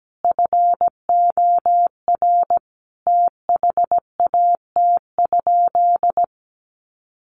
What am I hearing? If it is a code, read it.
Morse code, heard as a keyed tone, about 17 wpm, 700 Hz. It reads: FOR THAT?
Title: Sentences from Top 300 Words 17wpm